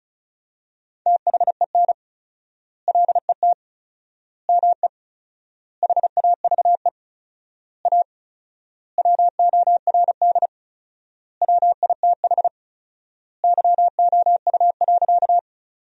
Morse code, heard as a keyed tone, about 34 wpm, 700 Hz. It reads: THEN LET ME HAVE A WORD WITH YOU.
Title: 4x Word-Spacing - Sentences from Top 300 Words - 35wpm